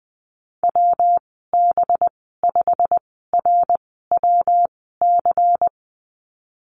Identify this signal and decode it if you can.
Morse code, heard as a keyed tone, about 21 wpm, spaced as for 15 wpm, 700 Hz. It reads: WB5RWC
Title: 193 Lesson 31g - Letter v - Call Signs 20wpm